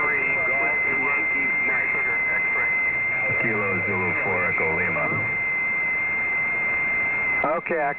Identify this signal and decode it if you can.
FT8, with speech.